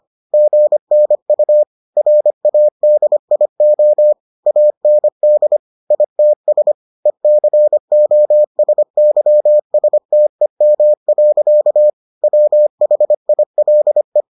Morse code, heard as a keyed tone, about 25 wpm, 600 Hz. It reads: GNU RADIO AND ITS ECOSYSTEM. WHILE